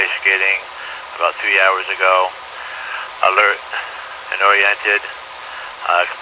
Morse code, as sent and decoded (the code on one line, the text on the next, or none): none